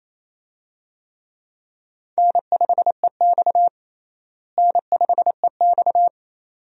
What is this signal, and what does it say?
Morse code, heard as a keyed tone, about 28 wpm, 700 Hz. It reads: N5EX N5EX
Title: USA Call Signs - 4 Characters - Repeated 2x - 28wpm